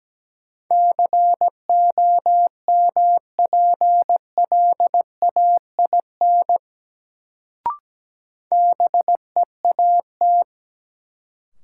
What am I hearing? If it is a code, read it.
Morse code, heard as a keyed tone, about 17 wpm, 700 Hz. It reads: COMPLAIN BEAT
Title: Sentences from Top 1300 Words - New Word Drill - 17wpm